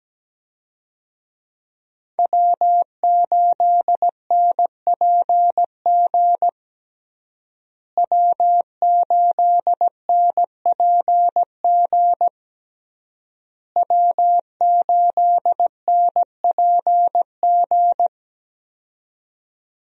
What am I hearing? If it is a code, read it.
Morse code, heard as a keyed tone, about 17 wpm, 700 Hz. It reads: W8NPG W8NPG W8NPG
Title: USA Call Signs - 5 Characters - Repeated 3x - 17wpm